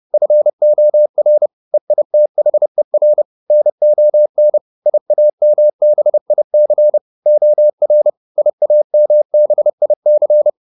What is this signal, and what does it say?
Morse code, heard as a keyed tone, about 30 wpm, 600 Hz. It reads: FOR EITHER NON IAMBIC OR IAMBIC
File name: SS301 30